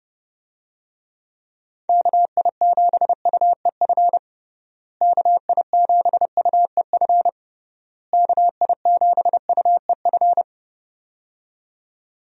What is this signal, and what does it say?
Morse code, heard as a keyed tone, about 30 wpm, 700 Hz. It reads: KI7UEF KI7UEF KI7UEF
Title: USA Call Signs - 6 Characters - Repeated 3x - 30wpm